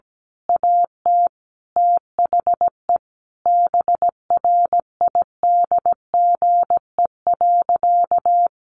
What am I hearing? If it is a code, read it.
Morse code, heard as a keyed tone, about 17 wpm, 700 Hz. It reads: AT THE BRIDGE.
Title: Sentences from Top 1200 Words 17wpm